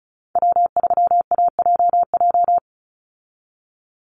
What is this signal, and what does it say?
Morse code, heard as a keyed tone, about 34 wpm, 700 Hz. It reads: W3AJJ
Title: USA Call Signs 35wpm